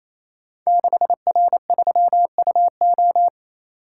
Morse code, heard as a keyed tone, about 29 wpm, 700 Hz. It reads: 6R3UO
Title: Code Groups - 5 Letter:Numbers - 28wpm